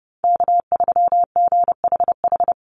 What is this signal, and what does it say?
Morse code, heard as a keyed tone, about 30 wpm, 700 Hz. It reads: K3GHH
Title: Call Signs - 30wpm